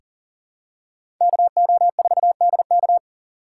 Morse code, heard as a keyed tone, about 40 wpm, 700 Hz. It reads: KO4DK
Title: USA Call Signs - 5 Characters - 6x Character Spacing - 40wpm